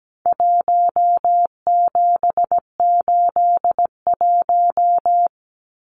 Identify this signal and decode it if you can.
Morse code, heard as a keyed tone, about 17 wpm, 700 Hz. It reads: 1781
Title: Numbers - Contest Style - 4-Digit 17wpm